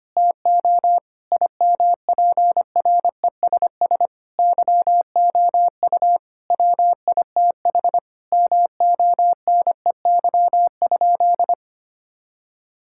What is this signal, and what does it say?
Morse code, heard as a keyed tone, about 25 wpm, 700 Hz. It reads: TO IMPRESS YOU WITH MONEY?
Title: Sentences from Top 1600 Words 25wpm